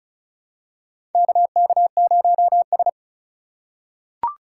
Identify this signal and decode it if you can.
Morse code, heard as a keyed tone, about 36 wpm, 700 Hz. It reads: KK0S
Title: USA Call Signs - 4 Characters - 2x Character Spacing - 35wpm